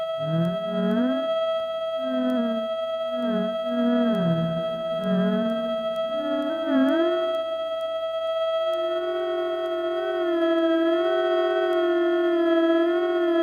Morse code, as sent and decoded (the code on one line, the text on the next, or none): none